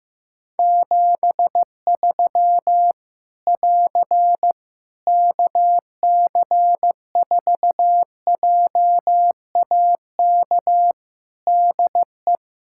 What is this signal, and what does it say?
Morse code, heard as a keyed tone, about 15 wpm, 700 Hz. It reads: 73 <AR> KC4JAK DE
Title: Practice QSO - Exchange 4 - Ending - 15wpm